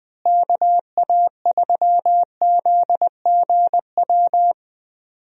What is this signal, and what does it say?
Morse code, heard as a keyed tone, about 20 wpm, 700 Hz. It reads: KA3ZGW
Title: USA Call Signs - 6 Characters - Repeated 3x - 20wpm